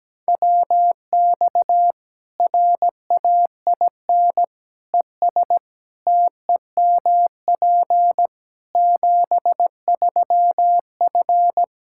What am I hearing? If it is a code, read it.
Morse code, heard as a keyed tone, about 17 wpm, 700 Hz. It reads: WX RAIN ES TEMP 73F
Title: Practice QSO - Exchange 2 - Info - 17wpm